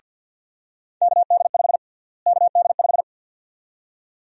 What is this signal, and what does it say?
Morse code, heard as a keyed tone, about 52 wpm, 700 Hz. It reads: KD5 KD5
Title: USA Call Sign Prefixes - Repeated 2x - 50wpm